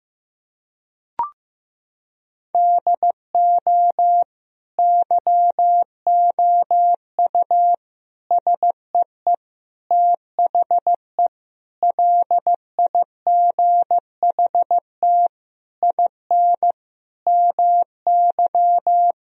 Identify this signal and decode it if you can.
Morse code, heard as a keyed tone, about 15 wpm, 700 Hz. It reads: DO YOU SEE THE LIGHT IN MY
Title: Sentences from Top 1000 Words - Review Part 1 - 15wpm